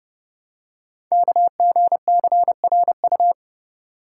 Morse code, heard as a keyed tone, about 28 wpm, 700 Hz. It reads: KGCRU
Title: Code Groups - 5 Letters - 30wpm